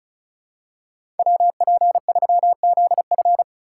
Morse code, heard as a keyed tone, about 34 wpm, 700 Hz. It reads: WP3ZF
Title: USA Call Signs - 5 Characters - 2x Character Spacing - 35wpm